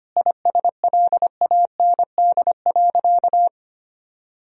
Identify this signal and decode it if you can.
Morse code, heard as a keyed tone, about 25 wpm, 700 Hz. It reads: ISLAND.
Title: Sentences from Top 1000 Words - Review Part 3 - 25wpm